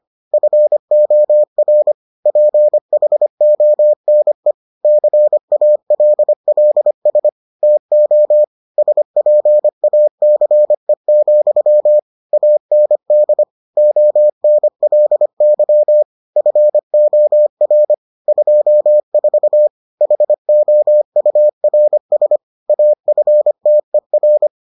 Morse code, heard as a keyed tone, about 25 wpm, 600 Hz. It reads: FOR PHONE CALLS TO SPACE, AND ONLY FOR 24 HOURS AFTER